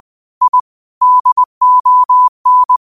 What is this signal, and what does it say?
Morse code, heard as a keyed tone, about 20 wpm, 1000 Hz. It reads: I DON